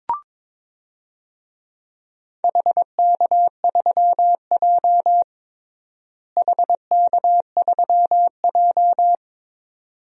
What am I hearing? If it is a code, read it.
Morse code, heard as a keyed tone, about 22 wpm, 700 Hz. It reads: HK3J HK3J
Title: DXCC-100 Call Signs - 4 Characters - Repeated 2x - 22wpm